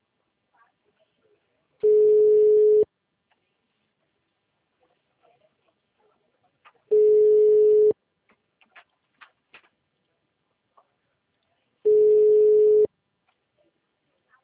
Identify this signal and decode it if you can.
Ringback tone (Europe, 425 Hz).
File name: out- -2016